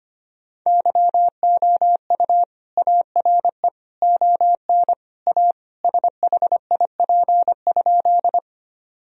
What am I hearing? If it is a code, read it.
Morse code, heard as a keyed tone, about 25 wpm, 700 Hz. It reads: YOU ARE ON A SHIP?
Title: Sentences from Top 1000 Words - Review Part 2 - 25wpm